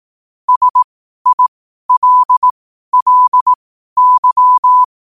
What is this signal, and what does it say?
Morse code, heard as a keyed tone, about 18 wpm, spaced as for 12 wpm, 1000 Hz. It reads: SILLY
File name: qotdfiles / 12wpm